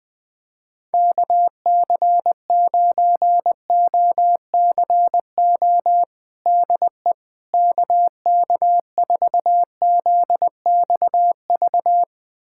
Morse code, heard as a keyed tone, about 20 wpm, 700 Hz. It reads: KC9OCO DE KK4ZXV
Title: USA Callsign DE Callsign 20wpm